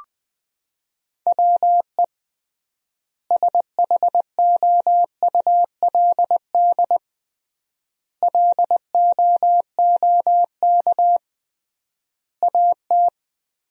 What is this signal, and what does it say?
Morse code, heard as a keyed tone, about 20 wpm, 700 Hz. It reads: WE SHOULD LOOK AT
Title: 3x Word-Spacing - Sentences from Top 300 Words - 20wpm